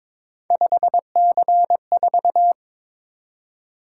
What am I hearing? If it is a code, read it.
Morse code, heard as a keyed tone, about 22 wpm, 700 Hz. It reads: 5C4
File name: practice / 22wpm / Three Letter-Number Combinations - 22wpm